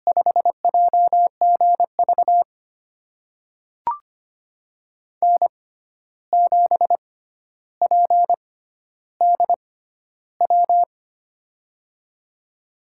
Morse code, heard as a keyed tone, about 25 wpm, spaced as for 12 wpm, 700 Hz. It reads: 5JGV N7PDW
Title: USA Call Signs - 5 Characters - 6x Character Spacing - 25wpm